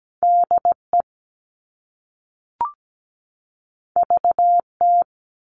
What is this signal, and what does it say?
Morse code, heard as a keyed tone, about 17 wpm, 700 Hz. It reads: DE VT
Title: US States 17wpm